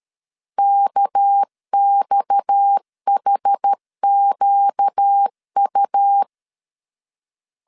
Morse code, heard as a keyed tone, about 13 wpm, 800 Hz. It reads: KXHQU
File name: CW / DET / 1 6 1500C S A 12wpm 3440